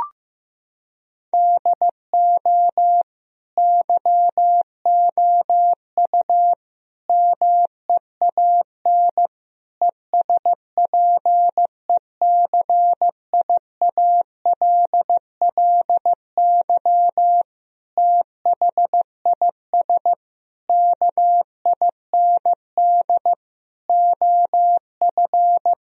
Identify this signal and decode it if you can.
Morse code, heard as a keyed tone, about 16 wpm, 700 Hz. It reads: DO YOU MEAN ESPECIALLY THIS KIND OF
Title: Sentences from Top 1000 Words - Review Part 4 - 15wpm